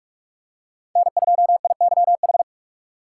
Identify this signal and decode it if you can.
Morse code, heard as a keyed tone, about 44 wpm, 700 Hz. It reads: N2IYH
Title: USA Call Signs - 5 Characters - 3x Character Spacing - 45wpm